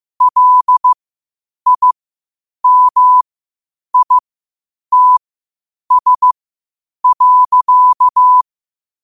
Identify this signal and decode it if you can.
Morse code, heard as a keyed tone, about 15 wpm, spaced as for 8 wpm, 1000 Hz. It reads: LIMITS.
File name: qotdfiles / 8wpm